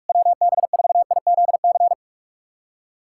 Morse code, heard as a keyed tone, about 41 wpm, 700 Hz. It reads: WB4IZC
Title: USA Callsign DE Callsign - 6 Characters - 45wpm